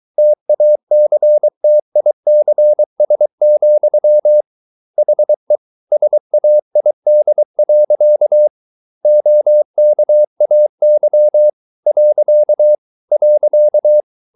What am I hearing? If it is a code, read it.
Morse code, heard as a keyed tone, about 23 wpm, 600 Hz. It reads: TACTICS, HE SAID. OKAY .